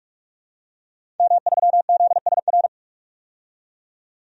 Morse code, heard as a keyed tone, about 46 wpm, 700 Hz. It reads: M3ZSR
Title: DXCC-100 Call Signs - 5 Characters - 45wpm